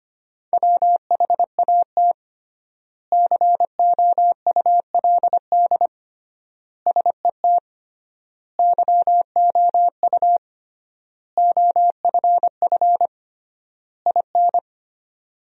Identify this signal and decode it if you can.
Morse code, heard as a keyed tone, about 25 wpm, 700 Hz. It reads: WHAT COULD SET YOU OFF IN
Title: 3x Word-Spacing - Sentences from Top 200 Words - 25wpm